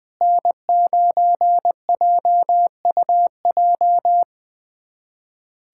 Morse code, heard as a keyed tone, about 21 wpm, 700 Hz. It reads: N9JUJ
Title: USA Callsign DE Callsign - 5 Characters - 20wpm